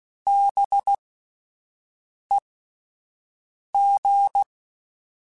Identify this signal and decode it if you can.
Morse code, heard as a keyed tone, about 16 wpm, spaced as for 5 wpm, 800 Hz. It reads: BEG
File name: Alphabetical / Alphabetical mp3s / Bs @5